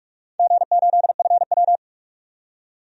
Morse code, heard as a keyed tone, about 47 wpm, 700 Hz. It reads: G8FW
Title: DXCC-100 Call Signs - 4 Characters - 45wpm